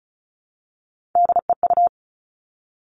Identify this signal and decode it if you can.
Morse code, heard as a keyed tone, about 36 wpm, 700 Hz. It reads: DEU
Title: Three Challenging Letter-Number Combinations 35wpm